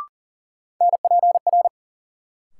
Morse code, heard as a keyed tone, about 37 wpm, 700 Hz. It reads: NPR
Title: Sets of 5 Words - New Word Drill - 40wpm